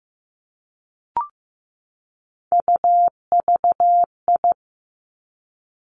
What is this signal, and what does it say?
Morse code, heard as a keyed tone, about 15 wpm, 700 Hz. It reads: UVI
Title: Three Challenging Letter-Number Combinations 15wpm